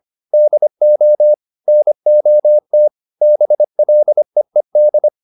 Morse code, heard as a keyed tone, about 25 wpm, 600 Hz. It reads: DO NOT BLEED